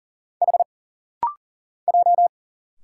Morse code, heard as a keyed tone, about 40 wpm, 700 Hz. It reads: H J
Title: Single Letters - Rapid-Fire 40wpm